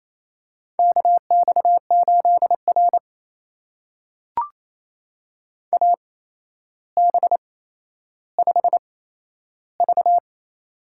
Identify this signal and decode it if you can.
Morse code, heard as a keyed tone, about 28 wpm, spaced as for 12 wpm, 700 Hz. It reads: KX8R AB5V
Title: USA Call Signs - 4 Characters - 8x Character Spacing - 28wpm